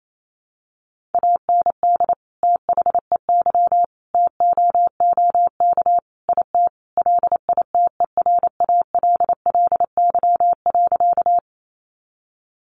Morse code, heard as a keyed tone, about 28 wpm, 700 Hz. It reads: AND THEY TOOK IT LITERALLY.
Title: Sentences from Top 1400 Words 28wpm